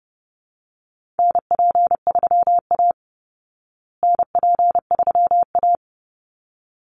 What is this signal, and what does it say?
Morse code, heard as a keyed tone, about 30 wpm, 700 Hz. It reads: NP3A NP3A
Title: Call Signs Repeated 3x - 30wpm